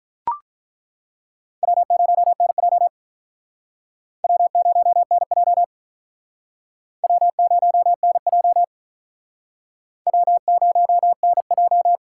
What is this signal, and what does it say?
Morse code, heard as a keyed tone, about 49 wpm, 700 Hz. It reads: W0NJ W0NJ W0NJ W0NJ